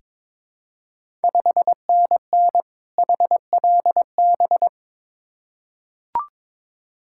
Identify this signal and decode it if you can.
Morse code, heard as a keyed tone, about 22 wpm, 700 Hz. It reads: 5NN HLB
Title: Ontario QSO Party Exchange 22wpm